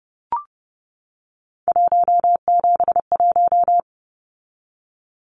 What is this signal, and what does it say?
Morse code, heard as a keyed tone, about 30 wpm, 700 Hz. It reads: 171
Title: Numbers - 3-Digit 30wpm